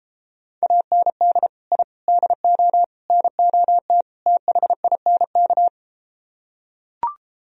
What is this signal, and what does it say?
Morse code, heard as a keyed tone, about 33 wpm, 700 Hz. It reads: AND I DO NOT THINK